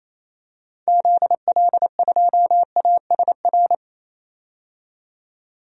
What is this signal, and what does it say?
Morse code, heard as a keyed tone, about 28 wpm, 700 Hz. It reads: ZL2ASR
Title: DXCC-100 Call Signs - 6 Characters - 28wpm